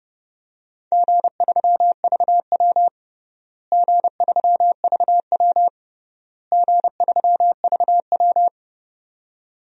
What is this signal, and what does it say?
Morse code, heard as a keyed tone, about 30 wpm, 700 Hz. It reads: G3VW G3VW G3VW
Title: DXCC-100 Call Signs - 4 Characters - Repeated 3x - 30wpm